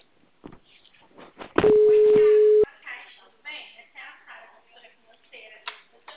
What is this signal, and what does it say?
Ringback tone (Europe, 425 Hz), with speech.
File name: out- -2013- - - .87443